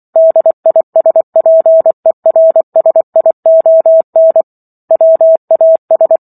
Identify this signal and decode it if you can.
Morse code, heard as a keyed tone, about 24 wpm, 650 Hz. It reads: DISPERSION WAS